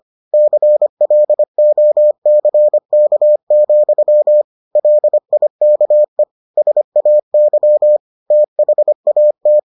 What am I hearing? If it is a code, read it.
Morse code, heard as a keyed tone, about 25 wpm, 600 Hz. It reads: CLOCK, LIKE SAY THAT